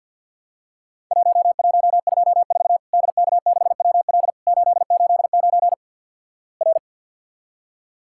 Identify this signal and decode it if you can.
Morse code, heard as a keyed tone, about 50 wpm, 700 Hz. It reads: 1124 DK6WL 889 R
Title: Worked All Europe DX Contest - QTC Exchange - 50wpm